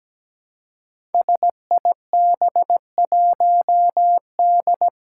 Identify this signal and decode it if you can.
Morse code, heard as a keyed tone, about 17 wpm, 700 Hz. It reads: SIB1D
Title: Five Challenging Letter-Number Combinations 17wpm